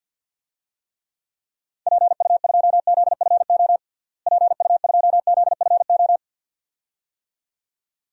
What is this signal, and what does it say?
Morse code, heard as a keyed tone, about 53 wpm, 700 Hz. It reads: PU2ZFO PU2ZFO
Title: DXCC-100 Call Signs - 6 Characters - Repeated 2x - 50wpm